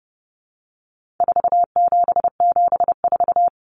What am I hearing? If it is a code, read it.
Morse code, heard as a keyed tone, about 30 wpm, 700 Hz. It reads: 4774
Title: Common Cut Numbers - 4-Digit 30wpm